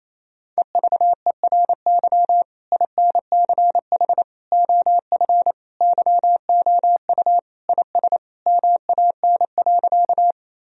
Morse code, heard as a keyed tone, about 28 wpm, 700 Hz. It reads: EVERY INCH OF YOU IS MAN.
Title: Sentences from Top 1000 Words - Review Part 3 - 28wpm